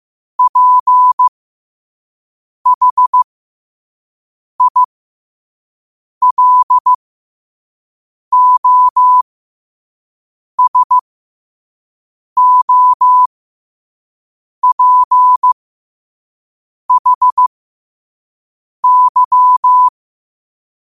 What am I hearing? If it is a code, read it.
Morse code, heard as a keyed tone, about 15 wpm, spaced as for 5 wpm, 1000 Hz. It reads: PHILOSOPHY